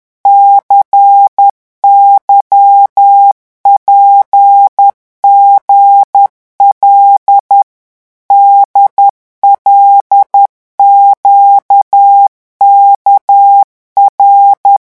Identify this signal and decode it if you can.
Morse code, heard as a keyed tone, about 10 wpm, 800 Hz. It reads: CYPGL DLQKR
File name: morse mp3 / revision5-vitesse 10 mots